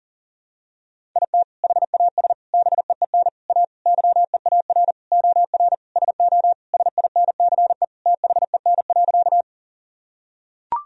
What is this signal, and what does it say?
Morse code, heard as a keyed tone, about 40 wpm, 700 Hz. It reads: IT HAS BEEN A YEAR OR SO SINCE THEN.
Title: Sentences from Top 1000 Words - Review Part 1 - 40wpm